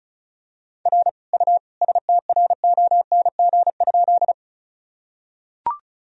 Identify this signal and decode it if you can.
Morse code, heard as a keyed tone, about 34 wpm, 700 Hz. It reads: R U STRONG?
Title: Sentences from Top 500 Words and Abbreviations 35wpm